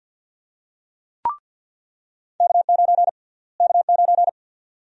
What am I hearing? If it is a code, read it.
Morse code, heard as a keyed tone, about 47 wpm, 700 Hz. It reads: K9 K9
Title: USA Call Sign Prefixes - Repeated 2x - 50wpm